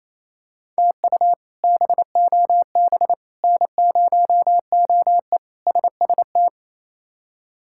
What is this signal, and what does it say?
Morse code, heard as a keyed tone, about 28 wpm, 700 Hz. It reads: TU BOB N0OE SST
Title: K1USN SST QSO Practice - 28wpm